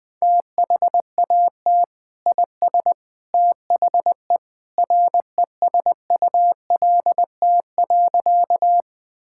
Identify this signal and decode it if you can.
Morse code, heard as a keyed tone, about 21 wpm, 700 Hz. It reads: THAT IS THE RESULT.
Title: Sentences from Top 1000 Words - Review Part 5 - 20wpm